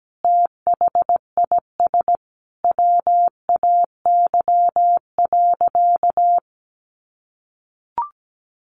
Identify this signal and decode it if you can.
Morse code, heard as a keyed tone, about 17 wpm, 700 Hz. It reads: THIS WAY.
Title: Sentences from Top 100 Words 17wpm